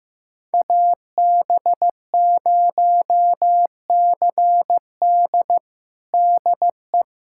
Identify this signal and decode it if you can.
Morse code, heard as a keyed tone, about 15 wpm, 700 Hz. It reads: AB0CD DE
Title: USA Callsign DE Callsign 15wpm